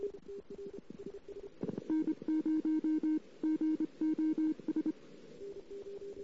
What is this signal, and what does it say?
Morse code, heard as a keyed tone, about 25 wpm, 310 Hz. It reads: N0GOS
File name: CW / 2008-ss-cw-k5tr-op-n5rz -17:00